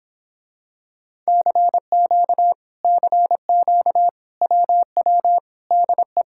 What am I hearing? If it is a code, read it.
Morse code, heard as a keyed tone, about 26 wpm, 700 Hz. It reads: CQ CQ WW DE